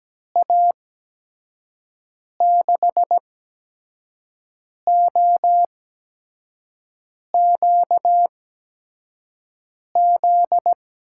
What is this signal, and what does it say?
Morse code, heard as a keyed tone, about 17 wpm, spaced as for 5 wpm, 700 Hz. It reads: A6OQZ
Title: USA Call Signs - 6 Characters - 8x Character Spacing - 17wpm